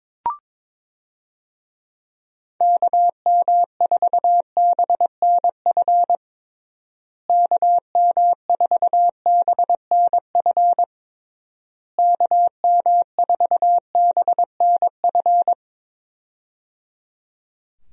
Morse code, heard as a keyed tone, about 22 wpm, 700 Hz. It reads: KM4BNF KM4BNF KM4BNF
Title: USA Call Signs - 6 Characters - Repeated 3x - 22wpm